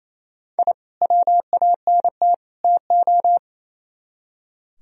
Morse code, heard as a keyed tone, about 35 wpm, 700 Hz. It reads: I WANT TO
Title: Speed-Racing - Sets of 3 Words 28wpm